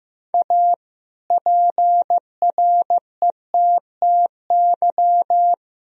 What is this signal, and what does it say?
Morse code, heard as a keyed tone, about 16 wpm, 700 Hz. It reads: A PRETTY
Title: Sentences from Top 1000 Words - Review Part 2 - 15wpm